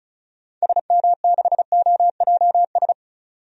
Morse code, heard as a keyed tone, about 34 wpm, 700 Hz. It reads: SM6OJS
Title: DXCC-100 Call Signs - 6 Characters - 35wpm